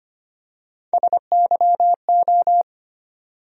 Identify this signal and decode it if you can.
Morse code, heard as a keyed tone, about 25 wpm, 700 Hz. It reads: SYO
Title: USA Call Sign Postfixes - 3 Letters 25wpm